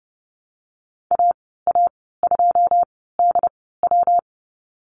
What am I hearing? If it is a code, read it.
Morse code, heard as a keyed tone, about 30 wpm, spaced as for 17 wpm, 700 Hz. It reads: AA2DW
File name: Lesson 19g - Letter w - Call Signs 30wpm